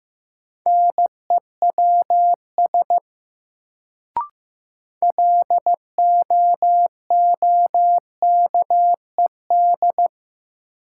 Morse code, heard as a keyed tone, about 15 wpm, 700 Hz. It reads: NEWS LOOKED
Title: Sentences from Top 800 Words - New Word Drill - 15wpm